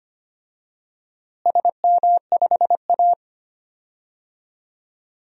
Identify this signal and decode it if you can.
Morse code, heard as a keyed tone, about 25 wpm, 700 Hz. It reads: SM5A
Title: DXCC-100 Call Signs - 4 Characters - Repeated 3x - 25wpm